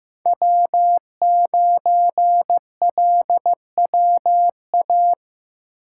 Morse code, heard as a keyed tone, about 15 wpm, 700 Hz. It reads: W9LWA
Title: USA Call Signs - 5 Characters - 4x Character Spacing - 15wpm